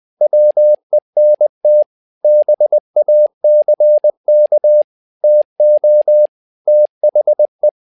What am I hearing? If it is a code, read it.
Morse code, heard as a keyed tone, about 20 wpm, 600 Hz. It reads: WENT BACK TO THE
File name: tower trouble 20wpm